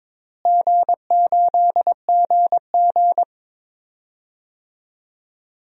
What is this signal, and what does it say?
Morse code, heard as a keyed tone, about 21 wpm, 700 Hz. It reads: G8GG
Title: DXCC-100 Call Signs - 4 Characters - Repeated 2x - 22wpm